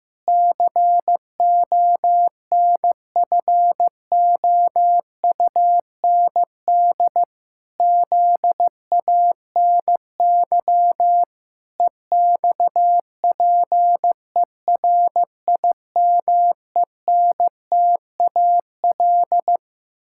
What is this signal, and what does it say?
Morse code, heard as a keyed tone, about 15 wpm, 700 Hz. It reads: CONFOUND ZANY EXPERIMENTAL
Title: Pangram Challenge v2 - 15wpm